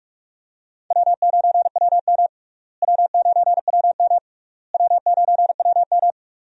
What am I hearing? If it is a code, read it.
Morse code, heard as a keyed tone, about 46 wpm, 700 Hz. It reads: W9WM W9WM W9WM
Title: USA Call Signs - 4 Characters - Repeated 3x - 45wpm